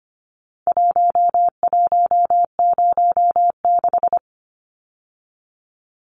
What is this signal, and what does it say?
Morse code, heard as a keyed tone, about 24 wpm, 700 Hz. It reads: 1106
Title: Numbers - Contest Style - 4-Digit 25wpm